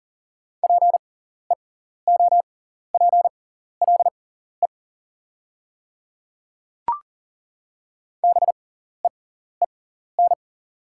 Morse code, heard as a keyed tone, about 37 wpm, spaced as for 14 wpm, 700 Hz. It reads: PEOPLE BEEN
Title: 6x Character Spacing - Top 100 Words - 40wpm